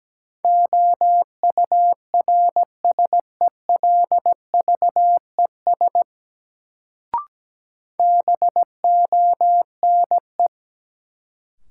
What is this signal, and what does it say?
Morse code, heard as a keyed tone, about 17 wpm, 700 Hz. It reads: OURSELVES BONE
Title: Sentences from Top 1600 Words - New Word Drill Alt - 17wpm